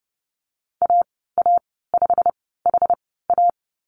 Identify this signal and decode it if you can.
Morse code, heard as a keyed tone, about 30 wpm, spaced as for 17 wpm, 700 Hz. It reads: AA5HA
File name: Lesson 19g - Letter w - Call Signs 30wpm